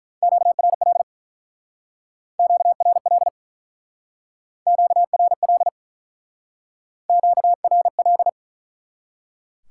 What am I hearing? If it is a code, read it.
Morse code, heard as a keyed tone, about 43 wpm, 700 Hz. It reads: QRL QRL QRL QRL
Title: Speed-Racing Top 180 QSO Elements 35wpm